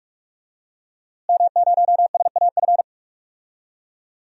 Morse code, heard as a keyed tone, about 44 wpm, 700 Hz. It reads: M0SAF
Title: DXCC-100 Call Signs - 5 Characters - Repeated 2x - 45wpm